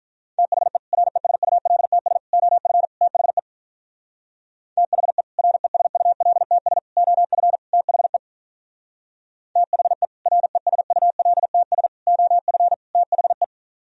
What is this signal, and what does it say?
Morse code, heard as a keyed tone, about 45 wpm, 700 Hz. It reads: THE RESULTS OF THE THE RESULTS OF THE THE RESULTS OF THE